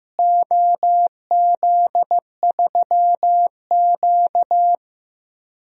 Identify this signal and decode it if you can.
Morse code, heard as a keyed tone, about 15 wpm, 700 Hz. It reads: OZ3Q
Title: DXCC-100 Call Signs - 4 Characters - Repeated 3x - 15wpm